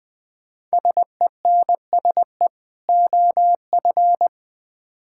Morse code, heard as a keyed tone, about 20 wpm, 700 Hz. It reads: SENSE OF
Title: Build A Phrase - 2-Word Phrases - 20wpm